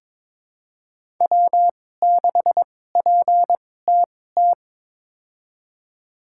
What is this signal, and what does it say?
Morse code, heard as a keyed tone, about 22 wpm, spaced as for 16 wpm, 700 Hz. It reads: W6PTT
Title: USA Call Signs - 5 Characters - 2x Character Spacing - 22wpm